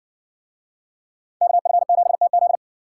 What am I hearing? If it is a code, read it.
Morse code, heard as a keyed tone, about 59 wpm, 700 Hz. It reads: KF7TZ